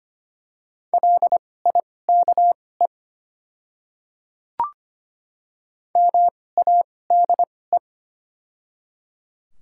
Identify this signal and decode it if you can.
Morse code, heard as a keyed tone, about 25 wpm, spaced as for 18 wpm, 700 Hz. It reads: LIKE MADE
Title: 2x Character Spacing - Top 500 Words - 25wpm